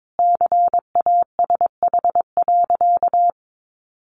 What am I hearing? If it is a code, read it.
Morse code, heard as a keyed tone, about 22 wpm, 700 Hz. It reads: CASH.
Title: Sentences from Top 1300 Words 22wpm